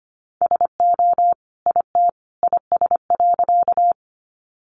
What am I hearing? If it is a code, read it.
Morse code, heard as a keyed tone, about 25 wpm, 700 Hz. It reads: SO IT IS.
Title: Sentences from Top 100 Words 25wpm